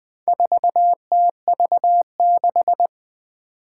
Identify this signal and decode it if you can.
Morse code, heard as a keyed tone, about 20 wpm, 700 Hz. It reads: 4TV6
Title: Four Challenging Letter-Number Combinations 20wpm